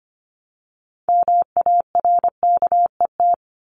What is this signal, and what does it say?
Morse code, heard as a keyed tone, about 25 wpm, 700 Hz. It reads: MARKET
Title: Top 501-750 Words 25wpm